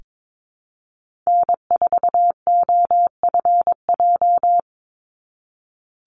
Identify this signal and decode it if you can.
Morse code, heard as a keyed tone, about 22 wpm, 700 Hz. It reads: N4OFJ
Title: USA Call Signs 22wpm